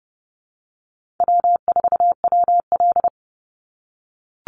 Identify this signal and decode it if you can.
Morse code, heard as a keyed tone, about 30 wpm, 700 Hz. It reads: W4WL
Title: USA Call Signs 30wpm